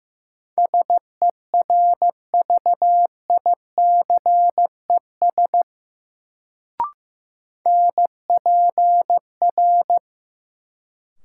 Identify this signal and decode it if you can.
Morse code, heard as a keyed tone, about 16 wpm, 700 Hz. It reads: SERVICES NPR
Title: Sets of 5 Words - New Word Drill - 15wpm